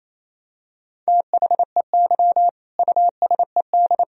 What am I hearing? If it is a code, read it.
Morse code, heard as a keyed tone, about 28 wpm, 700 Hz. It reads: THEY USED
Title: Sentences from Top 1000 Words - Review Part 4 - 28wpm